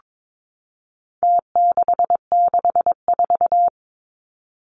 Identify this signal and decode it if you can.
Morse code, heard as a keyed tone, about 22 wpm, 700 Hz. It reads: T664
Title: Numbers - Contest Style - 4-Digit 22wpm